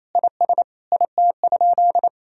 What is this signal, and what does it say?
Morse code, heard as a keyed tone, about 28 wpm, 700 Hz. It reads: IS IT?
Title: Sentences from Top 1000 Words - Review Part 3 - 28wpm